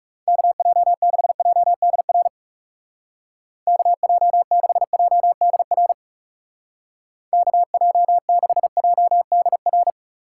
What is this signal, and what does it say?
Morse code, heard as a keyed tone, about 38 wpm, 700 Hz. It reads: KJ6JDR KJ6JDR KJ6JDR